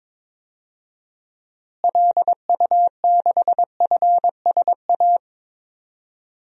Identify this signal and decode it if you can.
Morse code, heard as a keyed tone, about 22 wpm, 700 Hz. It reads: LU6FSA
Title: DXCC-100 Call Signs - 6 Characters - Repeated 3x - 22wpm